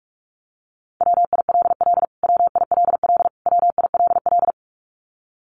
Morse code, heard as a keyed tone, about 44 wpm, 700 Hz. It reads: WILL WILL WILL
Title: Top 500 Words Repeated 45wpm